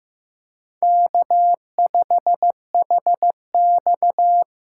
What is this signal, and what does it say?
Morse code, heard as a keyed tone, about 18 wpm, 700 Hz. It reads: K5HX
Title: Speed Racing - USA 1x2 Call Signs 15wpm